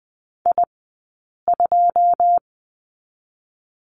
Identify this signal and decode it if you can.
Morse code, heard as a keyed tone, about 20 wpm, 700 Hz. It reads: I 2
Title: 080 Lesson 14d - Number 2 - Two Characters 20wpm